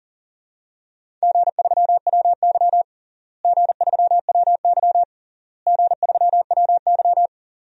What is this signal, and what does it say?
Morse code, heard as a keyed tone, about 40 wpm, 700 Hz. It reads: G3WY G3WY G3WY
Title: DXCC-100 Call Signs - 4 Characters - Repeated 3x - 40wpm